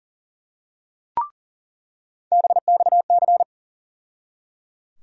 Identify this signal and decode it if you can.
Morse code, heard as a keyed tone, about 37 wpm, 700 Hz. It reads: BXC
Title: USA Call Sign Postfixes - 3 Letters 40wpm